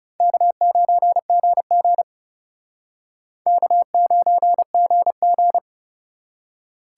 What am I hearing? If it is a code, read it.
Morse code, heard as a keyed tone, about 37 wpm, 700 Hz. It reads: K9GG K9GG
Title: Speed Racing - USA 1x2 Call Signs 30wpm